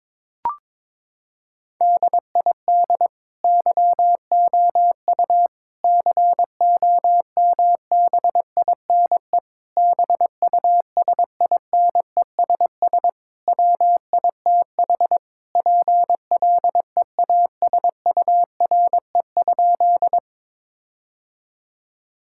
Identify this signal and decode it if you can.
Morse code, heard as a keyed tone, about 22 wpm, 700 Hz. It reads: DID YOU COMBINE BUSINESS WITH PLEASURE?
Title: Sentences from Top 1600 Words 22wpm